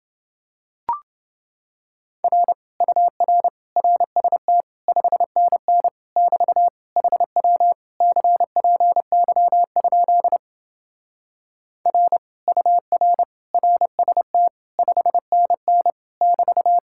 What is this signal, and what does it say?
Morse code, heard as a keyed tone, about 28 wpm, 700 Hz. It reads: R UR RST 5NN <BT> HW CPY? R UR RST 5NN <BT>